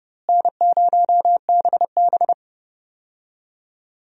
Morse code, heard as a keyed tone, about 32 wpm, 700 Hz. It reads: N0BB
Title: USA Call Signs - 4 Characters - Repeated 3x - 30wpm